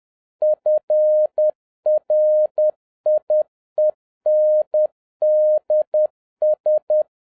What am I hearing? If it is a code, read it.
Morse code, heard as a keyed tone, about 10 wpm, 610 Hz. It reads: FRIENDS
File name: 2011-10-mp3 / n11-MEETING WITH GOOD FRIENDS 10wpm